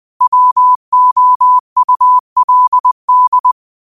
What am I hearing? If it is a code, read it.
Morse code, heard as a keyed tone, about 20 wpm, 1000 Hz. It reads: WOULD